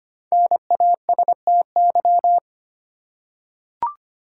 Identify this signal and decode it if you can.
Morse code, heard as a keyed tone, about 25 wpm, 700 Hz. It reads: NASTY
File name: 5 Letter Words - First 3 Letters Given - 25wpm